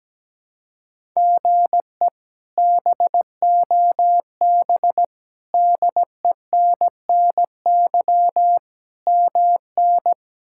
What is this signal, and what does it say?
Morse code, heard as a keyed tone, about 17 wpm, 700 Hz. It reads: GE BOB DENNY MN
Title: K1USN SST QSO Practice - 17wpm